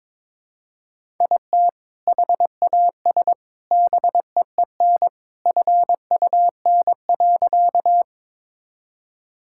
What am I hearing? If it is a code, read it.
Morse code, heard as a keyed tone, about 21 wpm, 700 Hz. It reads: IT HAS BEEN FUN.
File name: practice / 22wpm / Sentences with 3-Word Phrases - 22wpm